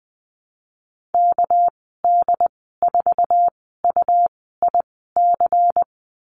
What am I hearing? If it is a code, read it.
Morse code, heard as a keyed tone, about 20 wpm, spaced as for 15 wpm, 700 Hz. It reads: KD4UIC
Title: 207 Lesson 33g - Letter j - Call Signs 20wpm